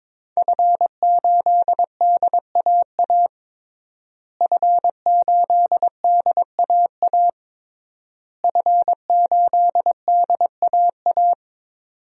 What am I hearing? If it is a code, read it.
Morse code, heard as a keyed tone, about 21 wpm, 700 Hz. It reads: F8DAA F8DAA F8DAA
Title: DXCC-100 Call Signs - 5 Characters - Repeated 3x - 22wpm